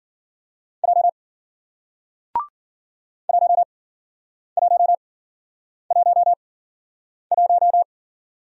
Morse code, heard as a keyed tone, about 54 wpm, spaced as for 9 wpm, 700 Hz. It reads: J 1111